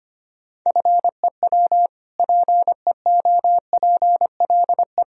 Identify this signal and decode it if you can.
Morse code, heard as a keyed tone, about 25 wpm, 700 Hz. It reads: FEW PEOPLE
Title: Sentences from Top 1000 Words - Review Part 1 - 25wpm